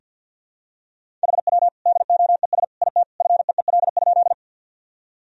Morse code, heard as a keyed tone, about 49 wpm, 700 Hz. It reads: HW DOES IT FEEL?
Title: Sentences from Top 400 Words and Abbreviations 50wpm